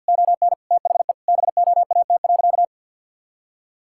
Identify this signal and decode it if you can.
Morse code, heard as a keyed tone, about 50 wpm, 700 Hz. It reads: ON THE BOAT.
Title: Sentences from Top 1000 Words - Review Part 5 - 50wpm